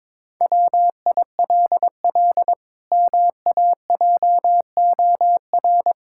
Morse code, heard as a keyed tone, about 22 wpm, 700 Hz. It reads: WILL MAJOR
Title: Pangram Challenge v2 - 22wpm